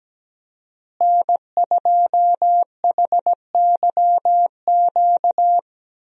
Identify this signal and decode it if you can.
Morse code, heard as a keyed tone, about 17 wpm, 700 Hz. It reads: N2HYQ
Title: USA Call Signs - 5 Characters - Repeated 2x - 17wpm